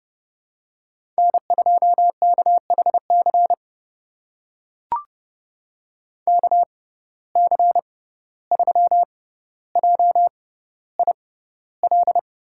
Morse code, heard as a keyed tone, about 30 wpm, spaced as for 15 wpm, 700 Hz. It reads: N2KHC KC3JIL
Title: USA Call Signs - 6x Character Spacing - 30wpm